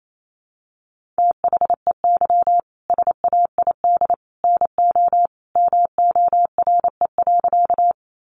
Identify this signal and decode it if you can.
Morse code, heard as a keyed tone, about 28 wpm, 700 Hz. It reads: THEY SAID NO MORE.
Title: Sentences from Top 100 Words 28wpm